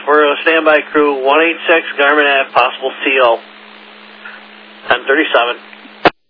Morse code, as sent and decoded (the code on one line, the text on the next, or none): none